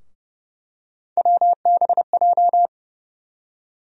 Morse code, heard as a keyed tone, about 29 wpm, 700 Hz. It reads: WBJ
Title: USA Call Sign Postfixes - 3 Letters 30wpm